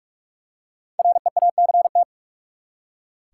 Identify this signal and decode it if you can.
Morse code, heard as a keyed tone, about 46 wpm, 700 Hz. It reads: REACT
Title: Five Letter Words - 45wpm